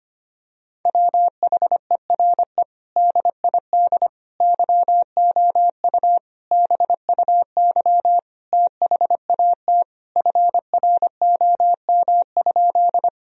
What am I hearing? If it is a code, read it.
Morse code, heard as a keyed tone, about 25 wpm, 700 Hz. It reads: WHERE DID YOU BUY THAT FROM?
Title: Sentences from Top 1000 Words - Review Part 4 - 25wpm